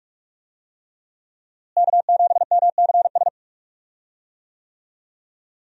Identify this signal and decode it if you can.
Morse code, heard as a keyed tone, about 46 wpm, 700 Hz. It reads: K7MCS
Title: USA Call Signs - 5 Characters - Repeated 3x - 45wpm